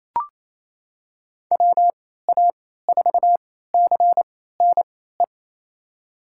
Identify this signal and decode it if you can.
Morse code, heard as a keyed tone, about 28 wpm, spaced as for 16 wpm, 700 Hz. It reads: WA4CNE
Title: USA Call Signs - 6 Characters - 3x Character Spacing - 28wpm